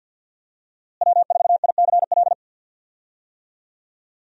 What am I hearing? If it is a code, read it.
Morse code, heard as a keyed tone, about 49 wpm, 700 Hz. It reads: W4ICL
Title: USA Call Signs - 5 Characters - Repeated 3x - 50wpm